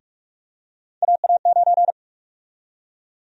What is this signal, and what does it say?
Morse code, heard as a keyed tone, about 47 wpm, 700 Hz. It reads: AA9
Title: USA Call Sign Prefixes - Repeated 2x - 45wpm